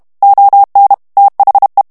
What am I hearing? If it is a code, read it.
Morse code, heard as a keyed tone, about 32 wpm, 800 Hz. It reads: ON THE
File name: morse mp3 / ANGLAIS texte clair 3 vitesse 30 mots minute